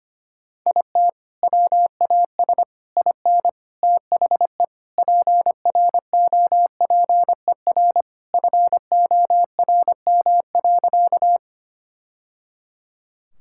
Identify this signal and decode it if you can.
Morse code, heard as a keyed tone, about 25 wpm, 700 Hz. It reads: IT WAS IN THE PROPER FORM.
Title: Sentences from Top 1000 Words - Review Part 5 - 25wpm